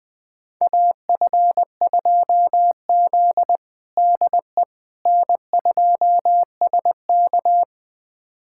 Morse code, heard as a keyed tone, about 21 wpm, 700 Hz. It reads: AF2Z DE N2SK
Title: USA Callsign DE Callsign - 4 Characters - 20wpm